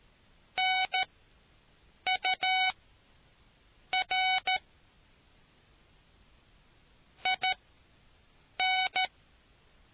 Morse code, heard as a keyed tone, about 13 wpm, spaced as for 6 wpm, 750 Hz. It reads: NUR IN